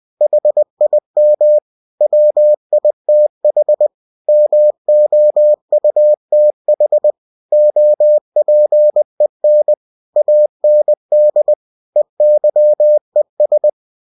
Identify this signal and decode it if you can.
Morse code, heard as a keyed tone, about 20 wpm, 600 Hz. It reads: HIM WITH MOUTH OPEN AND EYES